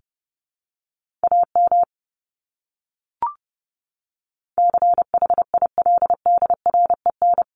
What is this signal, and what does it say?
Morse code, heard as a keyed tone, about 30 wpm, 700 Hz. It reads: AM CHILDREN
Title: Top 500 Words Repeated 30wpm